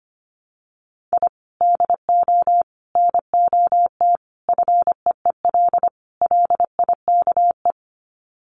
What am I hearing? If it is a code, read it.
Morse code, heard as a keyed tone, about 25 wpm, 700 Hz. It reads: I DO NOT FEEL LIKE
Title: Sets of 5 Words Encore 25wpm